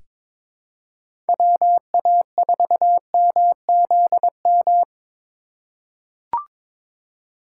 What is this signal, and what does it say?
Morse code, heard as a keyed tone, about 22 wpm, 700 Hz. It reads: WA4MZM
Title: USA Call Signs - 6 Characters - 3x Character Spacing - 22wpm